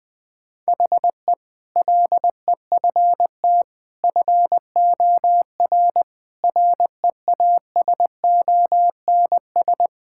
Morse code, heard as a keyed tone, about 21 wpm, 700 Hz. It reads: HE LEFT FOR REASONS
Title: Sentences with 8-Word Phrases - 20wpm